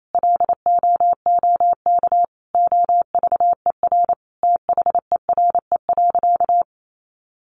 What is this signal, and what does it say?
Morse code, heard as a keyed tone, about 28 wpm, 700 Hz. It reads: LOOK OVER THERE.
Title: Sentences from Top 800 Words 28wpm